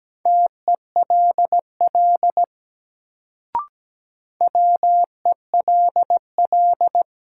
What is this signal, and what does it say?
Morse code, heard as a keyed tone, about 17 wpm, 700 Hz. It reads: TELL WELL
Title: Sets of 2 Words - New Word Drill - 17wpm